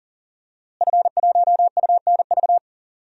Morse code, heard as a keyed tone, about 40 wpm, 700 Hz. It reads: F1UNV
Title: DXCC-100 Call Signs - 5 Characters - Repeated 3x - 40wpm